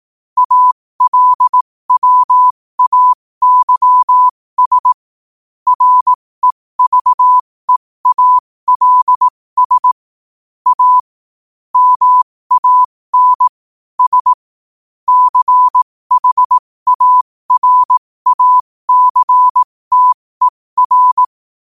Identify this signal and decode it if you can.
Morse code, heard as a keyed tone, about 18 wpm, spaced as for 15 wpm, 1000 Hz. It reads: ALWAYS REVEALS A MAN S CHARACTER
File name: qotdfiles / 15wpm